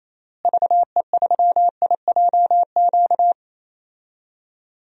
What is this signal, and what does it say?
Morse code, heard as a keyed tone, about 28 wpm, 700 Hz. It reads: VE3IJQ
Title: DXCC-100 Call Signs - 6 Characters - Repeated 2x - 28wpm